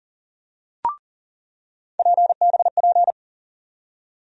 Morse code, heard as a keyed tone, about 40 wpm, 700 Hz. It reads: PBP
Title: USA Call Sign Postfixes - 3 Letters 40wpm